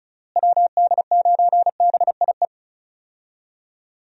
Morse code, heard as a keyed tone, about 34 wpm, 700 Hz. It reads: WD9BIE
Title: USA Call Signs - 6 Characters - Repeated 3x - 35wpm